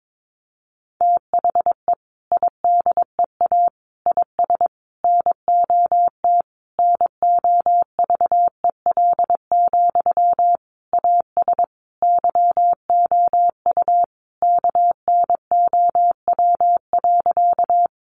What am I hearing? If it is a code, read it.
Morse code, heard as a keyed tone, about 22 wpm, 700 Hz. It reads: THE IDEA IS NOT NOVEL, AS YOU KNOW.
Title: Sentences from Top 1300 Words 22wpm